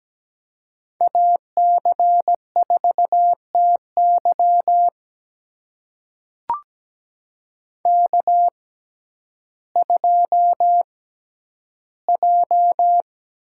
Morse code, heard as a keyed tone, about 17 wpm, spaced as for 10 wpm, 700 Hz. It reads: AC4TY K2J
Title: USA Call Signs - 5 Characters - 6x Character Spacing - 17wpm